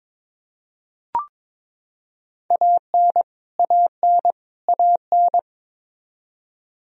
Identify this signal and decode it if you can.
Morse code, heard as a keyed tone, about 23 wpm, 700 Hz. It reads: AN AN AN
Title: Top 100 Words Repeated 22wpm